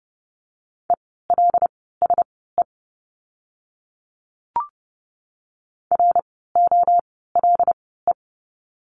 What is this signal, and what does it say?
Morse code, heard as a keyed tone, about 30 wpm, spaced as for 17 wpm, 700 Hz. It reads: ELSE ROLE
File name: Lesson 17f - Letter u - Words 30wpm